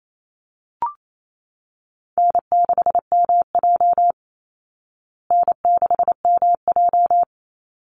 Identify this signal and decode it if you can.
Morse code, heard as a keyed tone, about 28 wpm, 700 Hz. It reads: N6MJ N6MJ
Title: Easy Call Signs Repeated 3x - 28wpm